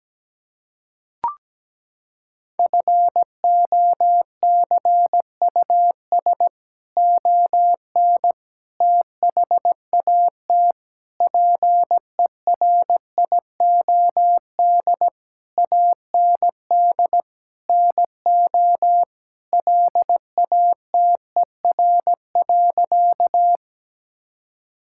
Morse code, heard as a keyed tone, about 17 wpm, 700 Hz. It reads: FOCUS ON THAT PERIOD AND NO LATER.
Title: Sentences from Top 1000 Words - Review Part 5 - 17wpm